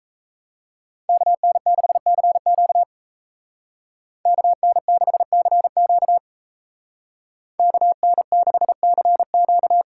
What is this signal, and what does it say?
Morse code, heard as a keyed tone, about 37 wpm, 700 Hz. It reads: KN6CQ KN6CQ KN6CQ